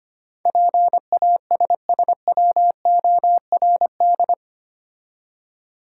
Morse code, heard as a keyed tone, about 25 wpm, 700 Hz. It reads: PASSWORD
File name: 8 Letter Words - First 3 Letters Given - 25wpm